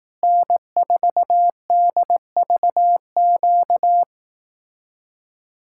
Morse code, heard as a keyed tone, about 18 wpm, 700 Hz. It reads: N4DVQ
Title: Speed-Racing Common QSO Phrases Encore - 15wpm